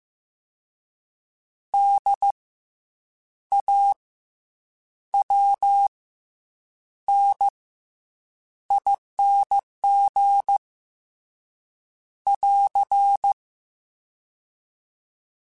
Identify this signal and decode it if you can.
Morse code, heard as a keyed tone, about 15 wpm, spaced as for 7 wpm, 800 Hz. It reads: DAWNING <AR>
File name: Special INGs-mp3s / Special INGs-mp3s / Fast ing @13